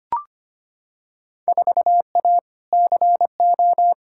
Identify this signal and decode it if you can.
Morse code, heard as a keyed tone, about 25 wpm, 700 Hz. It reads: 4A CO
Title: Field Day Exchanges - 25wpm